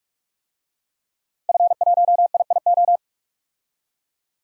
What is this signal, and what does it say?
Morse code, heard as a keyed tone, about 42 wpm, 700 Hz. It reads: F1IIO
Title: DXCC-100 Call Signs - 5 Characters - 45wpm